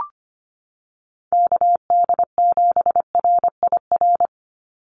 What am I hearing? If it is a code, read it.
Morse code, heard as a keyed tone, about 25 wpm, 700 Hz. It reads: KD7RIR
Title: USA Call Signs 25wpm